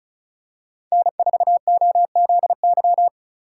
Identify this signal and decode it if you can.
Morse code, heard as a keyed tone, about 36 wpm, 700 Hz. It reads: N4OZY
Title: USA Call Signs - 5 Characters - 4x Character Spacing - 35wpm